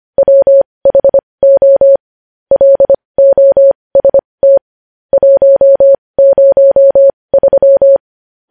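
Morse code, heard as a keyed tone, about 25 wpm, spaced as for 20 wpm, 550 Hz. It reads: WHO LOST 103
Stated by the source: Morse code at 20 wpm.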